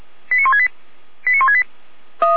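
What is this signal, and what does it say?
Five-tone selective call (ZVEI), heard as quick runs of five tones, tones of about 70 ms. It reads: 89168 89168
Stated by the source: SDR-received ZVEI five-tone call 89160.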